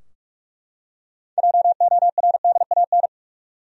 Morse code, heard as a keyed tone, about 41 wpm, 700 Hz. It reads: JORDAN
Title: USA Common Names - 45wpm